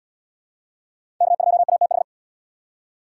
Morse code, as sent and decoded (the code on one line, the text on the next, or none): -.- ..--- .- - ....
K2ATH